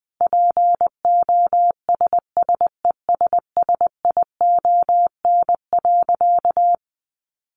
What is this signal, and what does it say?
Morse code, heard as a keyed tone, about 20 wpm, 700 Hz. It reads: POSSESSION.
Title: Sentences from Top 1400 Words 20wpm